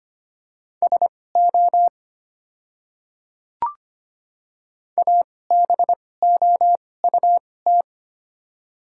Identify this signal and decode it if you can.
Morse code, heard as a keyed tone, about 25 wpm, spaced as for 18 wpm, 690 Hz. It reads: SO ABOUT
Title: 2x Character Spacing - Top 100 Words - 25wpm